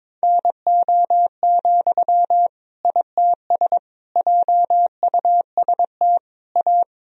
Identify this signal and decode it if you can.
Morse code, heard as a keyed tone, about 22 wpm, 700 Hz. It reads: NO, ITS JUST A
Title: Sentences from Top 1000 Words - Review Part 3 - 22wpm